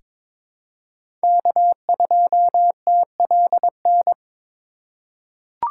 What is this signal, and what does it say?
Morse code, heard as a keyed tone, about 22 wpm, 700 Hz. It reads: K2TLN
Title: USA Call Signs - 5 Characters - 4x Character Spacing - 22wpm